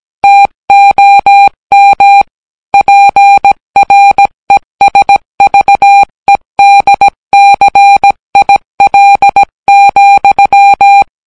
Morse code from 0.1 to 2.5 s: TOM